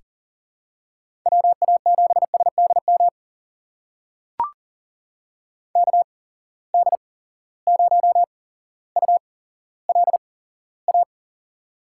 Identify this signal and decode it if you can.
Morse code, heard as a keyed tone, about 40 wpm, spaced as for 17 wpm, 700 Hz. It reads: WA7SDM KD0ULA
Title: USA Call Signs - 6 Characters - 8x Character Spacing - 40wpm